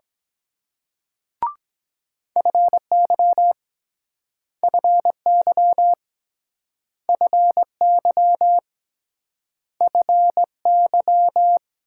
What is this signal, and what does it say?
Morse code, heard as a keyed tone, about 24 wpm, 700 Hz. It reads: FY FY FY FY